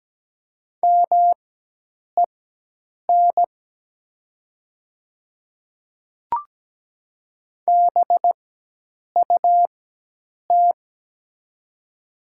Morse code, heard as a keyed tone, about 17 wpm, spaced as for 8 wpm, 690 Hz. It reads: MEN BUT
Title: 4x Character Spacing - Top 200 Words - 17wpm